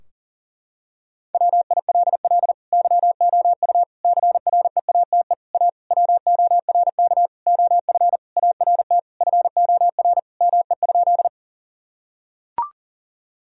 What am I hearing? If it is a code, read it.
Morse code, heard as a keyed tone, about 40 wpm, 700 Hz. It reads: WILL YOU CREATE A WORK OF ART FOR ME?
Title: Sentences from Top 1000 Words - Review Part 3 - 40wpm